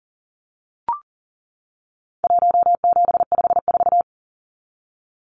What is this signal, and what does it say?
Morse code, heard as a keyed tone, about 40 wpm, 700 Hz. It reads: 1754
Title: Numbers - Contest Style - 4-Digit 40wpm